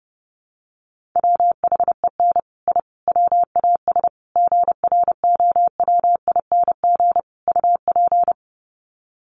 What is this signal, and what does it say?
Morse code, heard as a keyed tone, about 30 wpm, 700 Hz. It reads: WHEN I WAS GROWING UP
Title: Sets of 5 Words Encore 30wpm